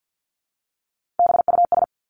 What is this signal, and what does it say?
Morse code, heard as a keyed tone, about 50 wpm, 700 Hz. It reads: BUS
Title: Top 751-1000 Words 50wpm